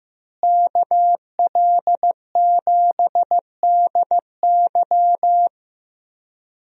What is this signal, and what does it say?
Morse code, heard as a keyed tone, about 15 wpm, 700 Hz. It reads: KL7DY
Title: DXCC-100 Call Signs - 5 Characters - Repeated 2x - 15wpm